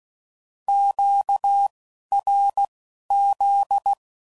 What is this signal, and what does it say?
Morse code, heard as a keyed tone, about 16 wpm, spaced as for 11 wpm, 800 Hz. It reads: QRZ
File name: CW Code Groups @10WPM / Q Calls / Q Calls Twice